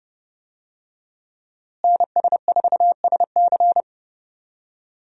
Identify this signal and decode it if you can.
Morse code, heard as a keyed tone, about 30 wpm, 700 Hz. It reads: NS4SC
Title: USA Call Signs - 5 Characters - Repeated 2x - 30wpm